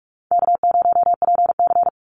Morse code, heard as a keyed tone, about 44 wpm, 700 Hz. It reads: K0PC
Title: Easy Call Signs Repeated 2x - 45wpm